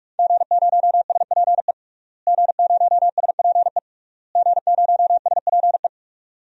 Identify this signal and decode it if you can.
Morse code, heard as a keyed tone, about 46 wpm, 700 Hz. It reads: G0SPE G0SPE G0SPE
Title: DXCC-100 Call Signs - 5 Characters - Repeated 3x - 45wpm